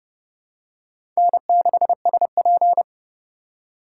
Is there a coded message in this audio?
Yes: Morse code from 1.2 to 2.8 s, heard as a keyed tone, about 29 wpm, 700 Hz. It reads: N6SP